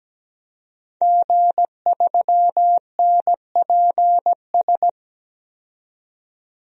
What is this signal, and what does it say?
Morse code, heard as a keyed tone, about 17 wpm, 700 Hz. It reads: G3NPS
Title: DXCC-100 Call Signs - 5 Characters - 17wpm